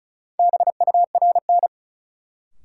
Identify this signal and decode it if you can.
Morse code, heard as a keyed tone, about 36 wpm, 700 Hz. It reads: BURN
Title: Sentences from Top 1300 Words - New Word Drill - 35wpm